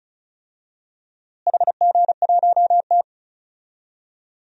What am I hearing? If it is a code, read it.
Morse code, heard as a keyed tone, about 36 wpm, 700 Hz. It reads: HG1T
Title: DXCC-100 Call Signs - 4 Characters - Repeated 3x - 35wpm